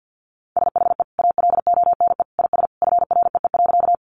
Morse code, heard as a keyed tone, about 50 wpm, 700 Hz. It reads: SHE ALONE IS FREE.
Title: Sentences from Top 1000 Words 50wpm